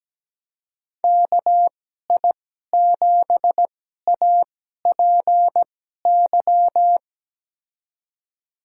Morse code, heard as a keyed tone, about 17 wpm, spaced as for 12 wpm, 700 Hz. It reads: KI7APY
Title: USA Call Signs - 6 Characters - 2x Character Spacing - 17wpm